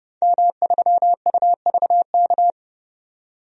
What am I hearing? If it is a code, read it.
Morse code, heard as a keyed tone, about 30 wpm, 700 Hz. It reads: M3UVK
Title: DXCC-100 Call Signs - 5 Characters - 30wpm